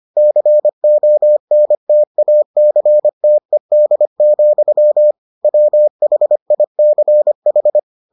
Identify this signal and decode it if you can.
Morse code, heard as a keyed tone, about 25 wpm, 600 Hz. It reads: CONTACTED, WHICH